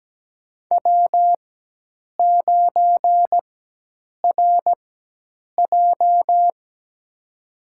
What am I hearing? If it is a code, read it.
Morse code, heard as a keyed tone, about 17 wpm, spaced as for 8 wpm, 700 Hz. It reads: W9RJ
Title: USA Call Signs - 4 Characters - 4x Character Spacing - 17wpm